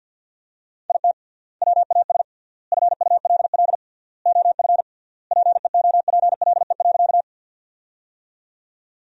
Morse code, heard as a keyed tone, about 50 wpm, 700 Hz. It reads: IT WAS FULL OF PEOPLE.
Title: 3x Word-Spacing - Sentences from Top 500 Words - 50wpm